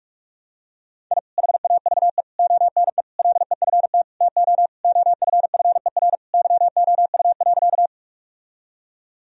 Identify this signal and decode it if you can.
Morse code, heard as a keyed tone, about 46 wpm, 700 Hz. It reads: I HAVE ONE LEFT TO OFFER YOU.
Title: Sentences from Top 1000 Words - Review Part 4 - 45wpm